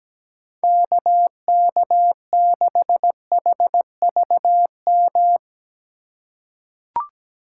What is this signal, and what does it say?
Morse code, heard as a keyed tone, about 17 wpm, 700 Hz. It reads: KK6HVM
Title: USA Call Signs - 6 Characters - 2x Character Spacing - 17wpm